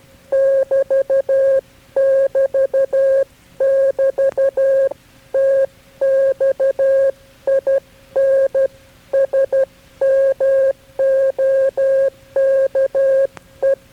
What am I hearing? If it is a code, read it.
Morse code, heard as a keyed tone, about 12 wpm, 540 Hz. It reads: <BT> <BT> <BT> TX IN SMOKE
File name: KB6MT-tape3-side1